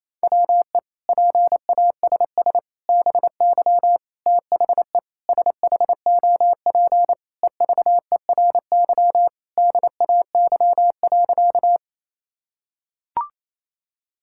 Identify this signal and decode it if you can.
Morse code, heard as a keyed tone, about 27 wpm, 700 Hz. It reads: WE PASS BY THE SHOP EVERY DAY.
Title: Sentences from Top 1600 Words 28wpm